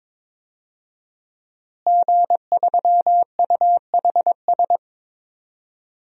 Morse code, heard as a keyed tone, about 22 wpm, 700 Hz. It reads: G3UHS
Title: DXCC-100 Call Signs - 5 Characters - Repeated 2x - 22wpm